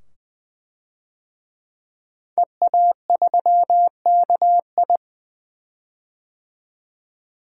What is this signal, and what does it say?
Morse code, heard as a keyed tone, about 20 wpm, 700 Hz. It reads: EA3KI
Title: DXCC-100 Call Signs - 5 Characters - 20wpm